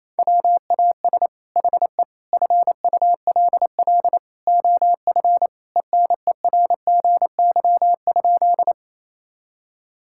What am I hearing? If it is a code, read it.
Morse code, heard as a keyed tone, about 28 wpm, 700 Hz. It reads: WAS HE FULL OF ENERGY?
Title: Sentences from Top 1000 Words - Review Part 6 - 28wpm